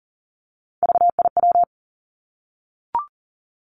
Morse code, heard as a keyed tone, about 42 wpm, 700 Hz. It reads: VIW
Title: Three Challenging Letter-Number Combinations 40wpm